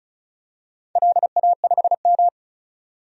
Morse code, heard as a keyed tone, about 36 wpm, 700 Hz. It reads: LA5M
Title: DXCC-100 Call Signs - 4 Characters - 35wpm